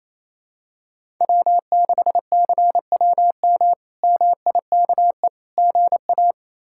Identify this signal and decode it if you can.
Morse code, heard as a keyed tone, about 28 wpm, 700 Hz. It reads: W6CWM MIKE GA
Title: K1USN SST QSO Practice - 28wpm